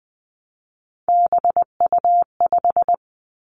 Morse code, heard as a keyed tone, about 20 wpm, 700 Hz. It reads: BU5
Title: Three Challenging Letter-Number Combinations 20wpm